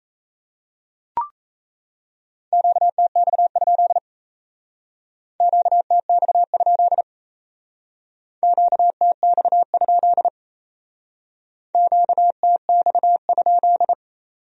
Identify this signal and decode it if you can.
Morse code, heard as a keyed tone, about 35 wpm, 700 Hz. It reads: QTX? QTX? QTX? QTX?
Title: Speed-Racing Q-Codes - 28wpm